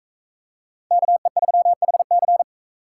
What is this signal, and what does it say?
Morse code, heard as a keyed tone, about 40 wpm, 700 Hz. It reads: KE3HC